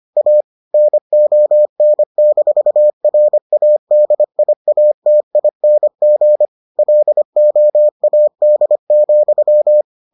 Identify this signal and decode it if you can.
Morse code, heard as a keyed tone, about 25 wpm, 600 Hz. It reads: A NON-RADIATING LOAD,